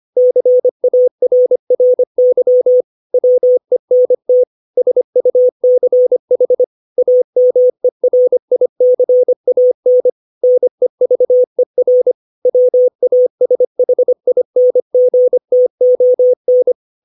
Morse code, heard as a keyed tone, about 25 wpm, 500 Hz. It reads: CARRY WENT SUCH AMERICAN NEVER WASHINGTON
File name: 632 words / 632 Words 25WPM